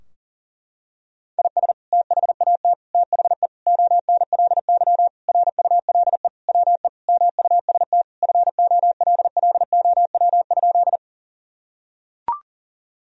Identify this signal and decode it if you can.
Morse code, heard as a keyed tone, about 40 wpm, 700 Hz. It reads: IS THAT THE ONLY RULE WE MUST FOLLOW?
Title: Sentences from Top 1000 Words - Review Part 2 - 40wpm